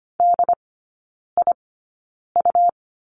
Morse code, heard as a keyed tone, about 25 wpm, spaced as for 9 wpm, 700 Hz. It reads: DIU
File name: Lesson 17e - Letter u - Three Characters 25wpm